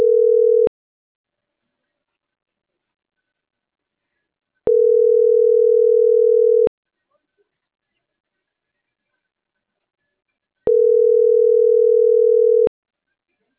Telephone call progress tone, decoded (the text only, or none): ringback tone